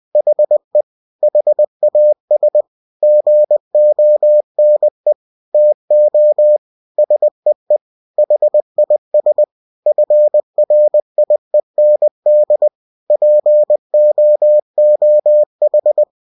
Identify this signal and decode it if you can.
Morse code, heard as a keyed tone, about 20 wpm, 600 Hz. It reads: HE HAS GONE TO SEE HIS FRIEND POOH
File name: cw-books / winnie-the-pooh / winnie-20wpm-page-0014